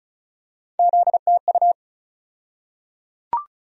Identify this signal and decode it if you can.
Morse code, heard as a keyed tone, about 34 wpm, 700 Hz. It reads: ZTU
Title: USA Call Sign Postfixes - 3 Letters 35wpm